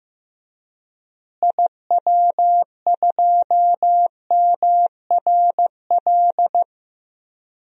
Morse code, heard as a keyed tone, about 15 wpm, 700 Hz. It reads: IW2MRL
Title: DXCC-100 Call Signs - 6 Characters - Repeated 3x - 15wpm